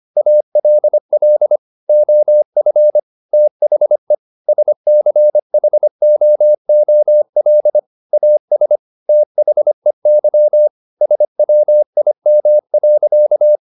Morse code, heard as a keyed tone, about 25 wpm, 600 Hz. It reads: ALL OF THE SCHOOL AS THEY SWIM.